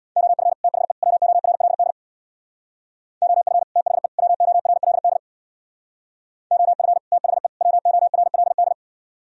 Morse code, heard as a keyed tone, about 60 wpm, 700 Hz. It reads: OF THE WORLD OF THE WORLD OF THE WORLD
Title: Speed-Racing - Sets of 3 Words 50wpm